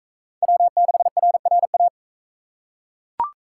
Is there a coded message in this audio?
Yes: Morse code from 0.4 to 1.9 s, heard as a keyed tone, about 39 wpm, 700 Hz. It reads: W6RRA